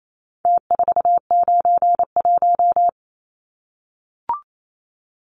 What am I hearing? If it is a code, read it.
Morse code, heard as a keyed tone, about 28 wpm, 700 Hz. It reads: T491
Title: Numbers - Contest Style - 4-Digit 28wpm